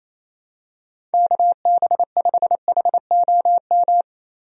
Morse code, heard as a keyed tone, about 27 wpm, 700 Hz. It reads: KB5HOM
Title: USA Call Signs - 6 Characters - Repeated 3x - 28wpm